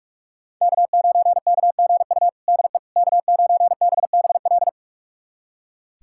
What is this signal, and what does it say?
Morse code, heard as a keyed tone, about 42 wpm, 700 Hz. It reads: K9KGU DE K9BBL
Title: USA Callsign DE Callsign - 5 Characters - 45wpm